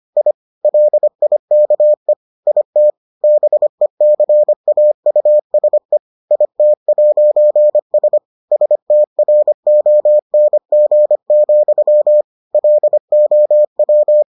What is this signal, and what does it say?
Morse code, heard as a keyed tone, about 25 wpm, 600 Hz. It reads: I LIKE IT BECAUSE IT'S STRONG, LOW